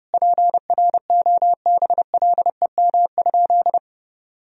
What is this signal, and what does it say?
Morse code, heard as a keyed tone, about 30 wpm, 700 Hz. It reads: PROBLEM?
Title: Sentences from Top 1000 Words - Review Part 3 - 30wpm